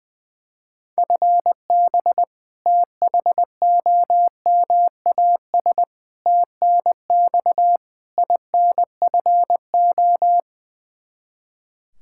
Morse code, heard as a keyed tone, about 21 wpm, 700 Hz. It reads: FB THOMAS TNX INFO
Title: Practice QSO - Exchange 3 - More Info - 20wpm